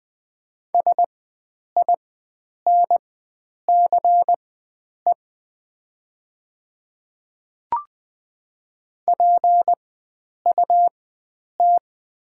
Morse code, heard as a keyed tone, about 20 wpm, spaced as for 9 wpm, 700 Hz. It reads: SINCE PUT
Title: 4x Character Spacing - Top 500 Words - 20wpm